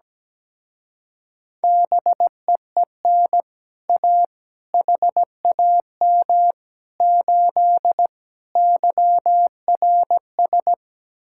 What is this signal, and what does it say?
Morse code, heard as a keyed tone, about 17 wpm, 700 Hz. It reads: BEEN A HAM 8 YRS
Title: Speed-Racing Common QSO Phrases - 17wpm